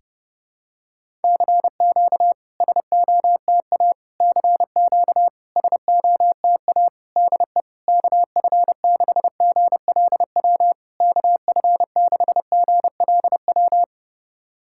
Morse code, heard as a keyed tone, about 30 wpm, 690 Hz. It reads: CQ SOTA CQ SOTA DE KF6GLW KF6GLW
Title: SOTA QSO Practice 30wpm